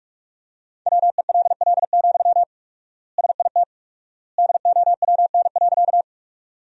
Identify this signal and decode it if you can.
Morse code, heard as a keyed tone, about 46 wpm, 700 Hz. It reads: WELL, SIT DOWN.
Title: 4x Word-Spacing - Sentences from Top 500 Words - 45wpm